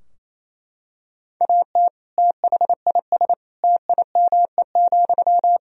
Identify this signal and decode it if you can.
Morse code, heard as a keyed tone, about 28 wpm, 700 Hz. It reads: AT THIS TIME,
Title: Sentences from Top 1000 Words - Review Part 2 - 28wpm